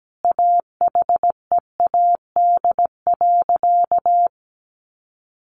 Morse code, heard as a keyed tone, about 17 wpm, 700 Hz. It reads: AHEAD.
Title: Sentences from Top 1100 Words 17wpm